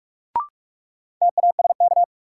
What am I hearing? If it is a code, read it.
Morse code, heard as a keyed tone, about 42 wpm, 700 Hz. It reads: TASK
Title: Sentences from Top 1000 Words - Review Part 7 - New Word Drill - 45wpm